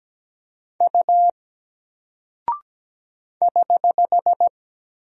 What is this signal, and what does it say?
Morse code, heard as a keyed tone, about 17 wpm, 700 Hz. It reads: U <HH>
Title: Single Character (Letter:Number:Punctuation:Prosign) Alt 17wpm